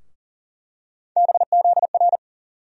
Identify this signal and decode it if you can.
Morse code, heard as a keyed tone, about 40 wpm, 700 Hz. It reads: BZR
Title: USA Call Sign Postfixes - 3 Letters 40wpm